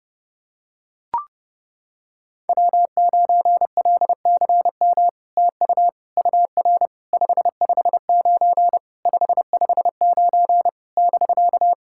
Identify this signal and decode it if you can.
Morse code, heard as a keyed tone, about 30 wpm, 700 Hz. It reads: W9LCM TU UR 559 559 <BK>
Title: SOTA QSO Practice 30wpm